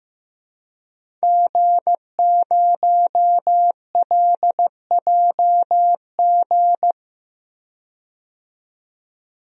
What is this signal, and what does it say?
Morse code, heard as a keyed tone, about 15 wpm, 700 Hz. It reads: G0LJG
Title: DXCC-100 Call Signs - 5 Characters - 15wpm